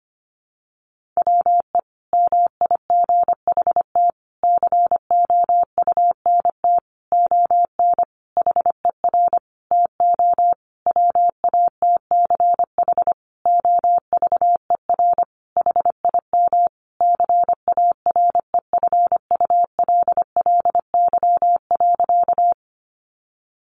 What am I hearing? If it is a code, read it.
Morse code, heard as a keyed tone, about 25 wpm, 700 Hz. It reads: WE MIGHT COUNT ON HER TO WATCH OVER HIM CAREFULLY.
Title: Sentences from Top 1500 Words 25wpm